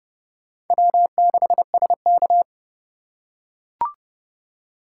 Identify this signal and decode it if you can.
Morse code, heard as a keyed tone, about 28 wpm, 700 Hz. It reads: W6SK